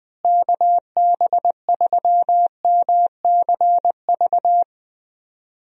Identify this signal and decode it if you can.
Morse code, heard as a keyed tone, about 20 wpm, 700 Hz. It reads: KB3MCV
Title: USA Call Signs - 6 Characters - Repeated 2x - 20wpm